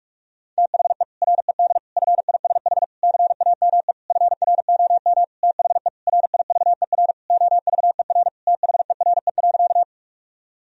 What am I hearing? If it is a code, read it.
Morse code, heard as a keyed tone, about 41 wpm, 700 Hz. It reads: THE RED FISH CAME FROM THE RIVER OVER THERE.
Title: Sentences from Top 1000 Words - Review Part 2 - 45wpm